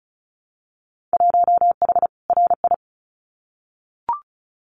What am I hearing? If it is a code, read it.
Morse code, heard as a keyed tone, about 34 wpm, 700 Hz. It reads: 1H RI
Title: Winter Field Day Exchanges 35wpm